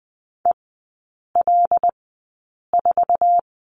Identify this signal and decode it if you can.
Morse code, heard as a keyed tone, about 20 wpm, spaced as for 8 wpm, 700 Hz. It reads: EL4
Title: 133 Lesson 22e - Question Mark - Three Characters 20wpm